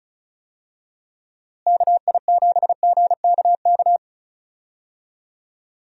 Morse code, heard as a keyed tone, about 34 wpm, 700 Hz. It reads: KI7GKK
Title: USA Call Signs - 6 Characters - Repeated 3x - 35wpm